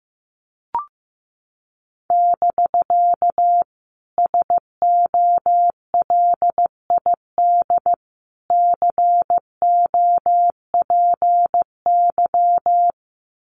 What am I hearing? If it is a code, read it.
Morse code, heard as a keyed tone, about 15 wpm, 700 Hz. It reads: <BK> SOLID COPY
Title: Common QSO Phrases Encore 15wpm